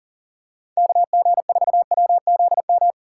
Morse code, heard as a keyed tone, about 40 wpm, 690 Hz. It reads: KG4WZM
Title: USA Call Signs - 4x Character Spacing - 40wpm